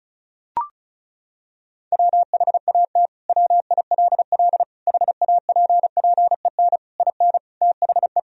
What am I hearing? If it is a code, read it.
Morse code, heard as a keyed tone, about 36 wpm, 700 Hz. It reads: WHAT WILL HAPPEN IN THE
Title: Sentences from Top 1000 Words - Review Part 2 - 35wpm